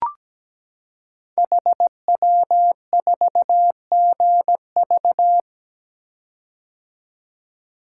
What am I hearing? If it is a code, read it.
Morse code, heard as a keyed tone, about 17 wpm, 690 Hz. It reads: HW4GV
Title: Code Groups - 5 Letter:Numbers - 17wpm